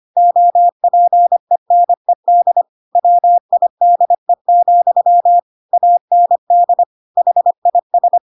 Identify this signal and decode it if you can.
Morse code, heard as a keyed tone, about 25 wpm, 700 Hz. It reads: OPENED WIDE, AND HIS
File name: Pinocchio / Pinocchio ch1 25 25 ik1hge